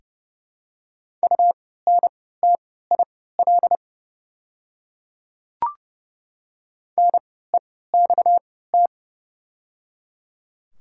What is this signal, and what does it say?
Morse code, heard as a keyed tone, about 30 wpm, spaced as for 17 wpm, 690 Hz. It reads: UNTIL NEXT
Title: 3x Character Spacing - Top 300 Words - 30wpm